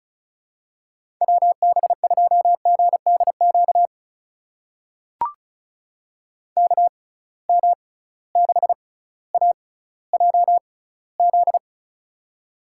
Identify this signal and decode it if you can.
Morse code, heard as a keyed tone, about 36 wpm, spaced as for 18 wpm, 700 Hz. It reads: WB2GDQ KM6AJZ
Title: USA Call Signs - 6 Characters - 6x Character Spacing - 35wpm